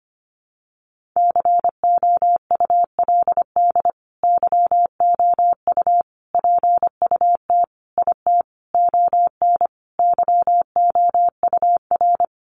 Morse code, heard as a keyed tone, about 25 wpm, 700 Hz. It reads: COULD YOU PUT IT ON YOUR
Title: Sentences from Top 1400 Words 25wpm